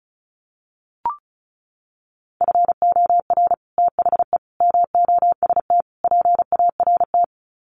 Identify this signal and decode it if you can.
Morse code, heard as a keyed tone, about 34 wpm, 700 Hz. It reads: FOR THE MOST PART
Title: Sets of 4 Words 35wpm